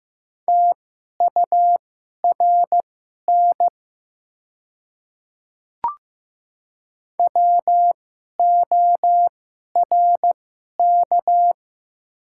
Morse code, heard as a keyed tone, about 15 wpm, spaced as for 11 wpm, 700 Hz. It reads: TURN WORK
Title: 2x Character Spacing - Top 200 Words - 15wpm